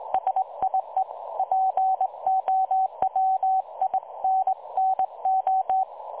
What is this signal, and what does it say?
Morse code, heard as a keyed tone, about 20 wpm, 760 Hz. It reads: SIE POWINNO
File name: dlazaawansowanych / nagraniazpasma / websdr recording 3565.0kHz